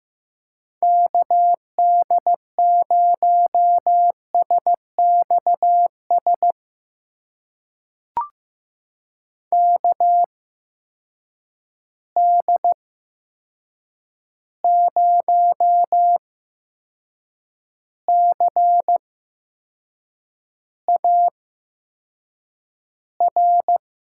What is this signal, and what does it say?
Morse code, heard as a keyed tone, about 15 wpm, spaced as for 6 wpm, 700 Hz. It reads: KD0SXS KD0CAR
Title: USA Call Signs - 6 Characters - 8x Character Spacing - 15wpm